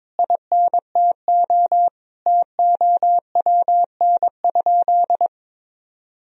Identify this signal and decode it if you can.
Morse code, heard as a keyed tone, about 21 wpm, 700 Hz. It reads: INTO TOWN?
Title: Sentences from Top 1000 Words - Review Part 2 - 22wpm